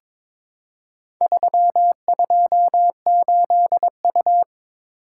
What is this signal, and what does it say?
Morse code, heard as a keyed tone, about 22 wpm, 690 Hz. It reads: 328U
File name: practice / 22wpm / Four Letter-Number Combinations - 22wpm